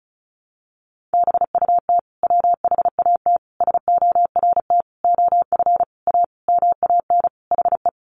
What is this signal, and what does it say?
Morse code, heard as a keyed tone, about 34 wpm, 700 Hz. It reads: BUT WHAT SORT OF A MAN HE
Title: Sentences from Top 1300 Words 35wpm